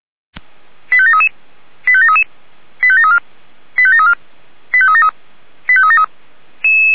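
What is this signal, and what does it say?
Five-tone selective call (ZVEI), heard as quick runs of five tones, tones of about 70 ms. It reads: 85733 85733 85734 85734 85373 85373
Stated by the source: ZVEI five-tone selective call received by SDR.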